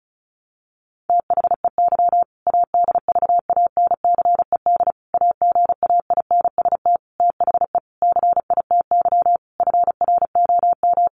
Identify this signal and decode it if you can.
Morse code, heard as a keyed tone, about 34 wpm, 700 Hz. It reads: THEY ADVANCED AGAINST THE CITY FROM
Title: Sentences from Top 1200 Words 35wpm